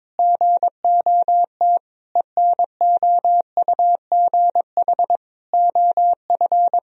Morse code, heard as a keyed tone, about 22 wpm, 700 Hz. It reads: GOT ENOUGH OF
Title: Sentences from Top 1000 Words - Review Part 3 - 22wpm